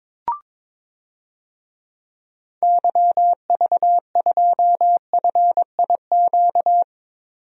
Morse code, heard as a keyed tone, about 22 wpm, 700 Hz. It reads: YV2FIQ
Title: DXCC-100 Call Signs - 6 Characters - 22wpm